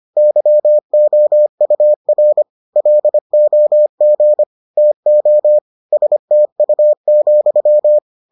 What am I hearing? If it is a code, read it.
Morse code, heard as a keyed tone, about 25 wpm, 600 Hz. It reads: YOUR LOG TO STU,